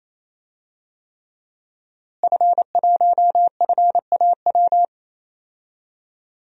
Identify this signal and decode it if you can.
Morse code, heard as a keyed tone, about 28 wpm, 690 Hz. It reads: F1FAW
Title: DXCC-100 Call Signs - 5 Characters - Repeated 3x - 28wpm